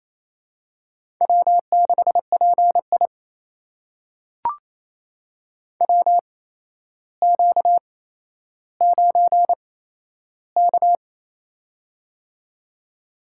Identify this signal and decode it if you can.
Morse code, heard as a keyed tone, about 27 wpm, spaced as for 12 wpm, 700 Hz. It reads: W6PI WQ9K
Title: USA Call Signs - 4 Characters - 8x Character Spacing - 28wpm